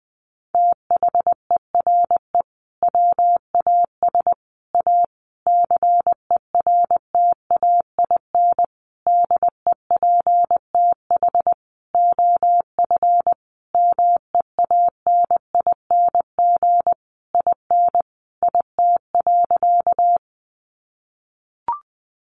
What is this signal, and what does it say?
Morse code, heard as a keyed tone, about 19 wpm, 700 Hz. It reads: THERE WAS A CERTAIN DEPTH OF MEANING IN IT.
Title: Sentences from Top 1200 Words 20wpm